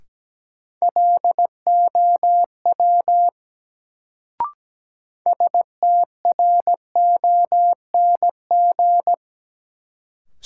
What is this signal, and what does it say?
Morse code, heard as a keyed tone, about 17 wpm, 700 Hz. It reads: LOW STRONG
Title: Sentences from Top 500 Words - New Word Drill - 17wpm